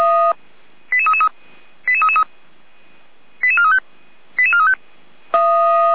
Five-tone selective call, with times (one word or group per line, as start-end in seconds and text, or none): 0.9-1.3 88333
1.9-2.2 88333
3.4-3.8 88435
4.4-4.7 88435